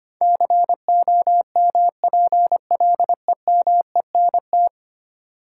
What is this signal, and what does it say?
Morse code, heard as a keyed tone, about 25 wpm, 700 Hz. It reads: COMPLEMENT
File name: Ten Letter Words - 25wpm